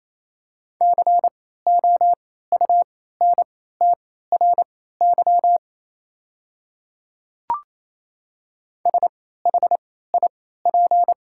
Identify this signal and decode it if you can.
Morse code, heard as a keyed tone, about 27 wpm, spaced as for 16 wpm, 700 Hz. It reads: COUNTRY SHIP
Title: 3x Character Spacing - Top 500 Words - 28wpm